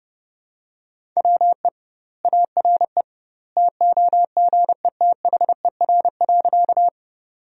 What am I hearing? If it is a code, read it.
Morse code, heard as a keyed tone, about 30 wpm, 700 Hz. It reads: WE ARE TOGETHER.
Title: 2x Word-Spacing - Sentences from Top 300 Words - 30wpm